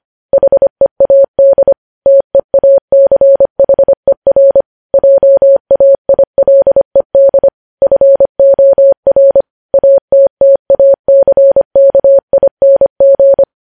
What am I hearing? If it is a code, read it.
Morse code, heard as a keyed tone, about 25 wpm, 550 Hz. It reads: HEAD TEACHER JAILED FOR ATTACKING
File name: podcast / News headlines-1x-25x25